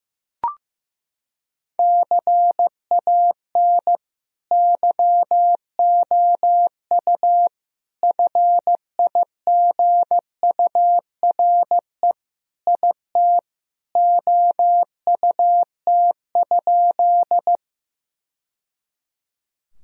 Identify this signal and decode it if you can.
Morse code, heard as a keyed tone, about 15 wpm, 700 Hz. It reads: CAN YOU FIGURE IT OUT?
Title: Sentences from Top 1000 Words - Review Part 2 - 15wpm